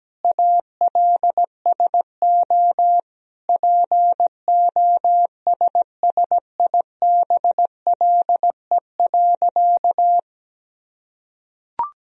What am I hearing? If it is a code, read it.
Morse code, heard as a keyed tone, about 17 wpm, 700 Hz. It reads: ALSO POSSIBLE.
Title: Sentences from Top 1000 Words - Review Part 1 - 17wpm